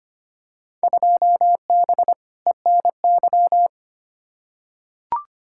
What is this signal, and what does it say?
Morse code, heard as a keyed tone, about 25 wpm, 700 Hz. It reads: 2B ENY
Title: Field Day Exchanges Encore - 25wpm